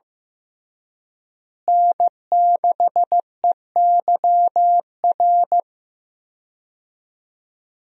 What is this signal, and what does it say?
Morse code, heard as a keyed tone, about 15 wpm, 700 Hz. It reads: N6EYR
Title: USA Call Signs - 5 Characters - Repeated 3x - 15wpm